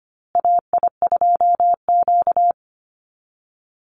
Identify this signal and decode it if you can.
Morse code, heard as a keyed tone, about 25 wpm, 700 Hz. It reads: AI2Q
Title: Call Signs Repeated 2x - 25wpm